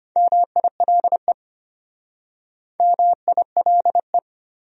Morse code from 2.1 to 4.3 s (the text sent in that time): MILE